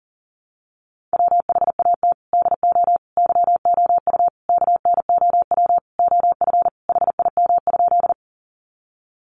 Morse code, heard as a keyed tone, about 40 wpm, 700 Hz. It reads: WHAT DO YOU KNOW OF HIM?
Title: Sentences from Top 100 Words 40wpm